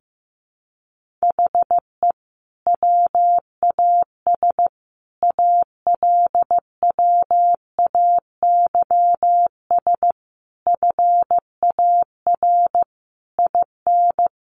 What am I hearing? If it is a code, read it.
Morse code, heard as a keyed tone, about 15 wpm, 700 Hz. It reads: HE WAS ALWAYS FAR IN
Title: Sentences from Top 1400 Words 15wpm